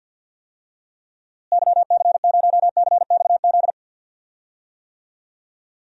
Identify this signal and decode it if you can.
Morse code, heard as a keyed tone, about 52 wpm, 700 Hz. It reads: YC0CXZ
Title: DXCC-100 Call Signs - 6 Characters - Repeated 2x - 50wpm